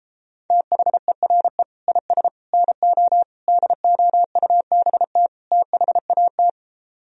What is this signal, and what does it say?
Morse code, heard as a keyed tone, about 35 wpm, 700 Hz. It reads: THERE IS NO DOUBT THAT
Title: Speed-Racing - Sets of 5 Words 28wpm